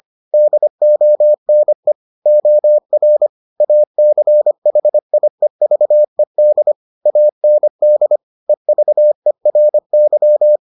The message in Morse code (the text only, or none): DONE OR ACHIEVED AND EVERY